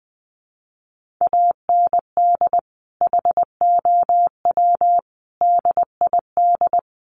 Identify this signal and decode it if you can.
Morse code, heard as a keyed tone, about 20 wpm, 700 Hz. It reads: AND HOW DID
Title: Sentences from Top 1400 Words 20wpm